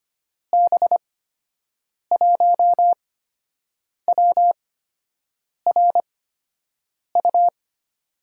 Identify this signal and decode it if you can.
Morse code, heard as a keyed tone, about 25 wpm, spaced as for 7 wpm, 700 Hz. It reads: B1WRU
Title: USA Call Signs - 6 Characters - 8x Character Spacing - 25wpm